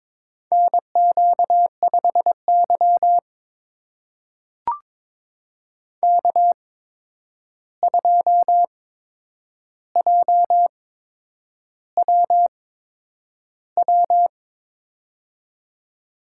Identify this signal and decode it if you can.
Morse code, heard as a keyed tone, about 22 wpm, spaced as for 9 wpm, 700 Hz. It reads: NQ5Y K2JWW
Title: USA Call Signs - 8x Character Spacing - 22wpm